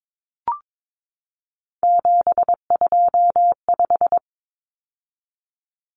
Morse code, heard as a keyed tone, about 22 wpm, 700 Hz. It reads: 725
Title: Common Cut Numbers - 3-Digit 22wpm